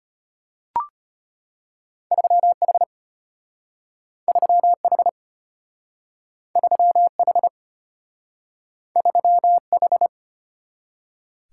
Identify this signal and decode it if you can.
Morse code, heard as a keyed tone, about 33 wpm, 700 Hz. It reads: 3H 3H 3H 3H